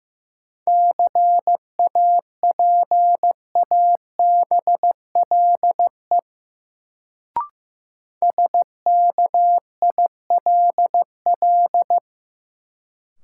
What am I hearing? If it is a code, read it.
Morse code, heard as a keyed tone, about 15 wpm, 700 Hz. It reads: CAPABLE SKILL
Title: Sentences from Top 1200 Words - New Word Drill - 15wpm